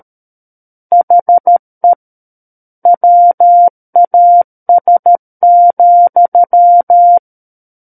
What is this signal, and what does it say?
Morse code, heard as a keyed tone, about 13 wpm, 700 Hz. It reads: HE WAS,
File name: Book One / 04-Chapter 4 - 13 WPM